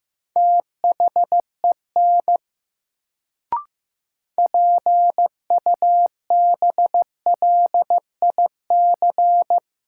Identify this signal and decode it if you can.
Morse code, heard as a keyed tone, about 15 wpm, 700 Hz. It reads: THEN PUBLIC
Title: Sets of 2 Words - New Word Drill - 15wpm